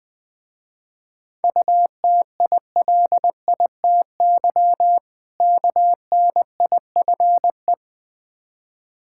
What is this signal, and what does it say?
Morse code, heard as a keyed tone, about 21 wpm, 700 Hz. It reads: UTILITY KNIFE
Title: Difficult Words with Hints - 20wpm